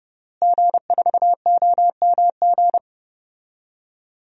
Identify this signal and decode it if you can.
Morse code, heard as a keyed tone, about 32 wpm, 700 Hz. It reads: G4OMG
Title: DXCC-100 Call Signs - 5 Characters - Repeated 3x - 30wpm